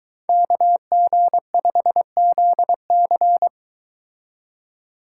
Morse code, heard as a keyed tone, about 23 wpm, 700 Hz. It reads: KG5ZC